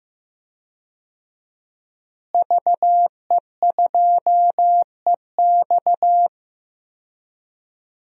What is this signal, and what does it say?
Morse code, heard as a keyed tone, about 15 wpm, 700 Hz. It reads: VE2EX
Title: DXCC-100 Call Signs - 5 Characters - Repeated 2x - 15wpm